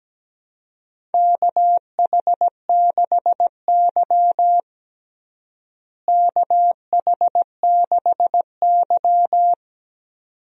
Morse code, heard as a keyed tone, about 17 wpm, 700 Hz. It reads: KH6Y KH6Y
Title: DXCC-100 Call Signs - 4 Characters - Repeated 2x - 17wpm